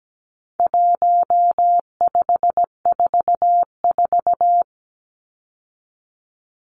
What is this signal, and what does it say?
Morse code, heard as a keyed tone, about 17 wpm, 700 Hz. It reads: 1544
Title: Numbers - 4-Digit 17wpm